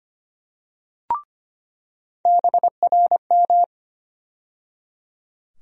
Morse code, heard as a keyed tone, about 25 wpm, 700 Hz. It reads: BRM
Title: USA Call Sign Postfixes - 3 Letters 25wpm